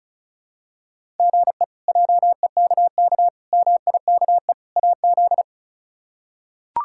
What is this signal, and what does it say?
Morse code, heard as a keyed tone, about 36 wpm, 700 Hz. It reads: GE JEKK MIKE AZ
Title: K1USN SST QSO Practice - 35wpm